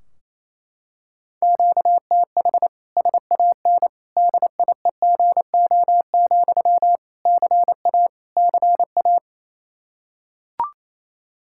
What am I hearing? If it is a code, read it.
Morse code, heard as a keyed tone, about 28 wpm, 700 Hz. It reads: QTH SAN DIEGO, CA CA
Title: Practice QSO - Exchange 1 - Beginning - 28wpm